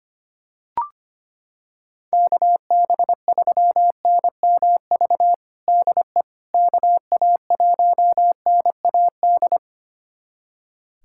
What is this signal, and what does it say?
Morse code, heard as a keyed tone, about 25 wpm, 700 Hz. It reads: KB3NMV DE KA1NAD
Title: USA Callsign DE Callsign - 6 Characters - 25wpm